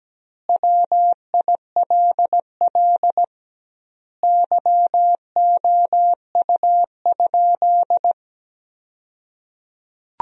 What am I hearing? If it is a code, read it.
Morse code, heard as a keyed tone, about 17 wpm, 700 Hz. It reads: WILL YOU?
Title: 2x Word-Spacing - Sentences from Top 400 Words - 17wpm